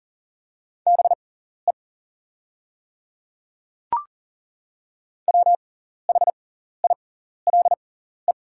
Morse code, heard as a keyed tone, about 40 wpm, spaced as for 14 wpm, 690 Hz. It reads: BE WHILE
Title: 6x Character Spacing - Top 300 Words - 40wpm